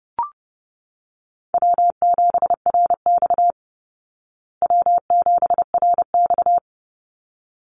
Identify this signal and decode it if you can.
Morse code, heard as a keyed tone, about 30 wpm, 700 Hz. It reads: W7RX W7RX
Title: Easy Call Signs Repeated 2x - 30wpm